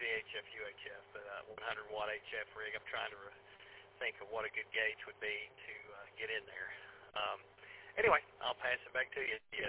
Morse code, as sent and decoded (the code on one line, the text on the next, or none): none